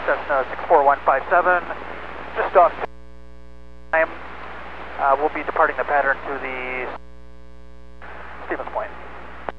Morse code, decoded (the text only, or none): none